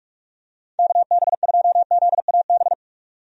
Morse code, heard as a keyed tone, about 46 wpm, 700 Hz. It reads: KB2ZAB
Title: USA Call Signs - 6 Characters - 45wpm